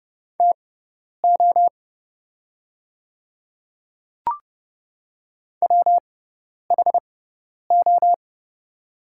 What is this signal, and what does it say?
Morse code, heard as a keyed tone, about 30 wpm, spaced as for 10 wpm, 700 Hz. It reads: TO WHO
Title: 6x Character Spacing - Top 300 Words - 30wpm